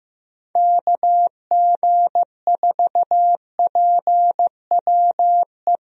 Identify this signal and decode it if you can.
Morse code, heard as a keyed tone, about 15 wpm, 700 Hz. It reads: KG4PWE
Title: USA Call Signs - 6 Characters - Repeated 3x - 15wpm